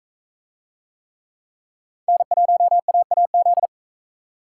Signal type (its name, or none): Morse code